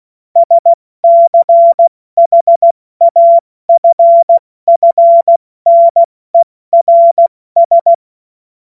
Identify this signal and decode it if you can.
Morse code, heard as a keyed tone, about 16 wpm, spaced as for 14 wpm, 680 Hz. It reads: SCHAFFNERS
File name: Morsekurs / Witze / Witz 05 Dezember 2023 70 F80